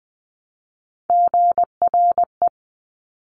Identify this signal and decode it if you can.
Morse code, heard as a keyed tone, about 20 wpm, 700 Hz. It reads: GRE
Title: Top Three Letter Combinations in Words 20wpm